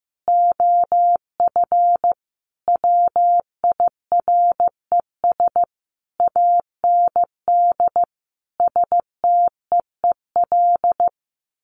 Morse code, heard as a keyed tone, about 14 wpm, 700 Hz. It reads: OF WIRES AND STEEL
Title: A Sound Of Thunder 15wpm